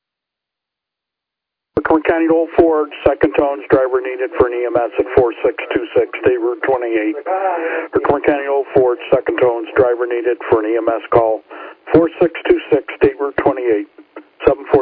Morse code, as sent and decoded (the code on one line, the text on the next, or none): none